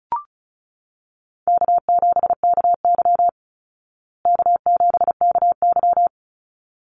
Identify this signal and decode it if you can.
Morse code, heard as a keyed tone, about 34 wpm, 700 Hz. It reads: K7KY K7KY
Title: Easy Call Signs Repeated 2x - 35wpm